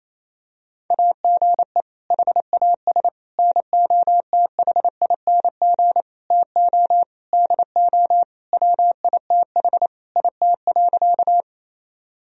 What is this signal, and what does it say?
Morse code, heard as a keyed tone, about 27 wpm, 700 Hz. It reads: AGE HAS NOTHING TO DO WITH IT.
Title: Sentences from Top 1000 Words - Review Part 3 - 28wpm